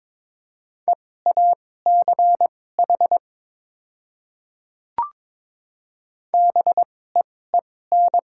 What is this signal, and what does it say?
Morse code, heard as a keyed tone, about 22 wpm, spaced as for 16 wpm, 700 Hz. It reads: EACH BEEN
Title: 2x Character Spacing - Top 100 Words - 22wpm